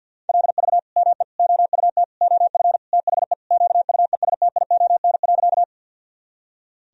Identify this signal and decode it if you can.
Morse code, heard as a keyed tone, about 52 wpm, 700 Hz. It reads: LV ME OUT OF THE QUESTION.
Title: Sentences from Top 400 Words and Abbreviations 50wpm